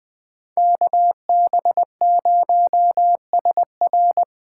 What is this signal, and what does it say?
Morse code, heard as a keyed tone, about 21 wpm, 700 Hz. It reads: KB0SR